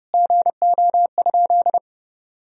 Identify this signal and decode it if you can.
Morse code, heard as a keyed tone, about 32 wpm, 700 Hz. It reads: GO?
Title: Sentences from Top 1000 Words - Review Part 4 - 30wpm